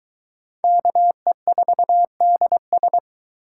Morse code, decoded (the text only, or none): KE4DS